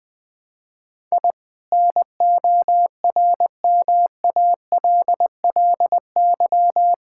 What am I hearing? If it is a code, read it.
Morse code, heard as a keyed tone, about 20 wpm, 700 Hz. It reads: I NORMALLY
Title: Sentences from Top 1000 Words - Review Part 6 - 20wpm